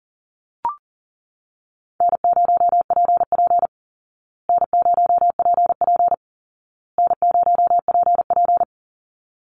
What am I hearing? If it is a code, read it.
Morse code, heard as a keyed tone, about 37 wpm, 700 Hz. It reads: N0PP N0PP N0PP
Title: Easy Call Signs Repeated 3x - 40wpm